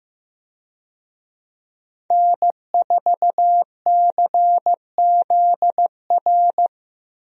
Morse code, heard as a keyed tone, about 15 wpm, 700 Hz. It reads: N4CZR
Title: USA Call Signs - 5 Characters - Repeated 2x - 15wpm